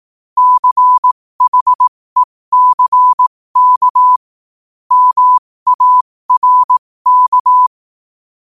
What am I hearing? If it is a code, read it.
Morse code, heard as a keyed tone, about 18 wpm, spaced as for 15 wpm, 1000 Hz. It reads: CHECK MARK
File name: qotdfiles / 15wpm